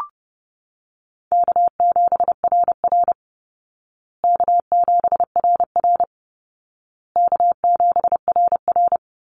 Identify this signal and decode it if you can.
Morse code, heard as a keyed tone, about 30 wpm, 700 Hz. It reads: K7RR K7RR K7RR
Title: Call Signs Repeated 2x - 30wpm